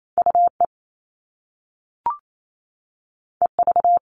Morse code, heard as a keyed tone, about 28 wpm, 700 Hz. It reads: UE EV
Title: All Cut Numbers - 2-Digit 28wpm